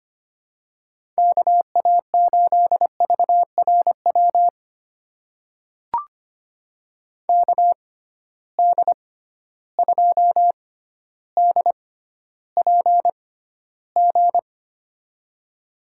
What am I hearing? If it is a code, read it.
Morse code, heard as a keyed tone, about 25 wpm, spaced as for 13 wpm, 700 Hz. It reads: KA8VRW KD2DPG
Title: USA Call Signs - 6 Characters - 6x Character Spacing - 25wpm